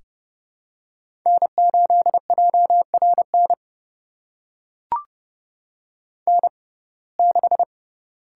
Morse code, heard as a keyed tone, about 30 wpm, spaced as for 22 wpm, 700 Hz. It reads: N8JRN N6
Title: USA Call Signs - 5 Characters - 6x Character Spacing - 30wpm